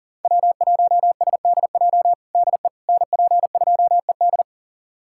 Morse code, heard as a keyed tone, about 40 wpm, 700 Hz. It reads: W1SDJ DE NP2ED
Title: USA Callsign DE Callsign - 5 Characters - 40wpm